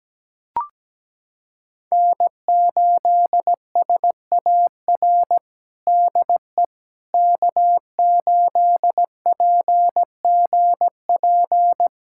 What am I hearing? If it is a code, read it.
Morse code, heard as a keyed tone, about 17 wpm, 700 Hz. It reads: N8SAR DE K8PGP
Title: USA Callsign DE Callsign - 5 Characters - 17wpm